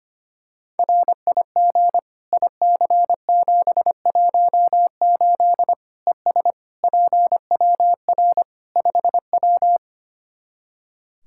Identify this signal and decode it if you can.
Morse code, heard as a keyed tone, about 25 wpm, 700 Hz. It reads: RIG IC718 ES PWR 5W
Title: Practice QSO - Exchange 2 - Info - 25wpm